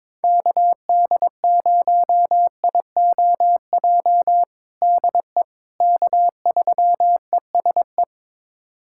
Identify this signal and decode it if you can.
Morse code, heard as a keyed tone, about 21 wpm, 700 Hz. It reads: KD0IOJ DE K3ESE
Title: USA Callsign DE Callsign 22wpm